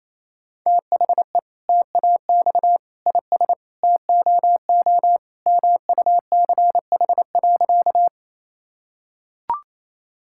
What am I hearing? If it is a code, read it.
Morse code, heard as a keyed tone, about 28 wpm, 700 Hz. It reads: THE TAX IS TOO MUCH.
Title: Sentences from Top 1000 Words - Review Part 5 - 28wpm